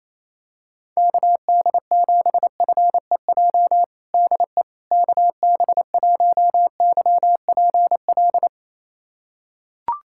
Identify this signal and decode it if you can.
Morse code, heard as a keyed tone, about 28 wpm, 700 Hz. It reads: KD7FEJ DE KB1YPL
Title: USA Callsign DE Callsign - 6 Characters - 28wpm